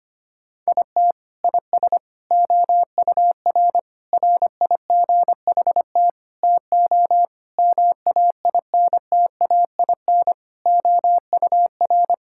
Morse code, heard as a keyed tone, about 25 wpm, 700 Hz. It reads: IT IS OUR RIGHT TO MAINTAIN OUR
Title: Sentences from Top 1000 Words - Review Part 5 - 25wpm